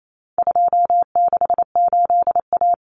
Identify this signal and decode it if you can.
Morse code, heard as a keyed tone, about 28 wpm, 700 Hz. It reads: 268A
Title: Common Cut Numbers - 4-Digit 28wpm